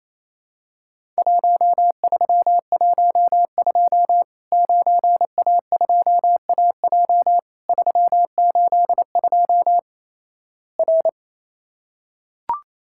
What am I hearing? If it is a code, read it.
Morse code, heard as a keyed tone, about 27 wpm, 700 Hz. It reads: 1312 9A2AJ 382 R
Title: Worked All Europe DX Contest - QTC Exchange - 28wpm